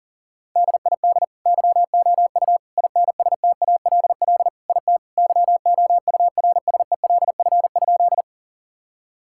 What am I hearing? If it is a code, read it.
Morse code, heard as a keyed tone, about 40 wpm, 700 Hz. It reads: DID YOU INSTALL IT YOURSELF?
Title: Sentences from Top 1600 Words 40wpm